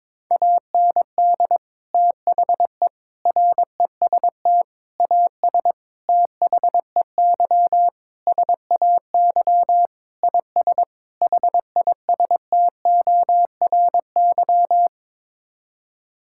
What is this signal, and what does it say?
Morse code, heard as a keyed tone, about 22 wpm, 700 Hz. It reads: AND THE REST AS THEY SAY IS HISTORY
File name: practice / 22wpm / Sets of 8 Words - 22wpm